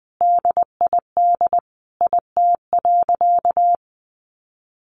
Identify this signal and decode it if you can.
Morse code, heard as a keyed tone, about 20 wpm, 700 Hz. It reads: DID IT.
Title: Sentences from Top 1500 Words 20wpm